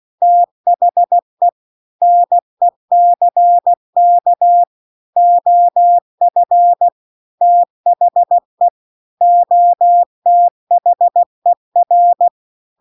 Morse code, heard as a keyed tone, about 16 wpm, 700 Hz. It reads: THE NECK OF THE OTHER